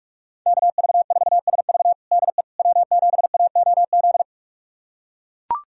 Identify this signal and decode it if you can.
Morse code, heard as a keyed tone, about 42 wpm, 700 Hz. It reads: KV4SV DE W7AOZ
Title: USA Callsign DE Callsign - 5 Characters - 45wpm